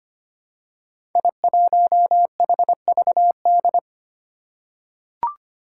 Morse code, heard as a keyed tone, about 25 wpm, 700 Hz. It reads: I1HVD
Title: Five Challenging Letter-Number Combinations 25wpm